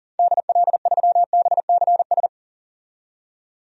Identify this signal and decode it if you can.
Morse code, heard as a keyed tone, about 40 wpm, 700 Hz. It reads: DL3BCS
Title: DXCC-100 Call Signs - 6 Characters - 40wpm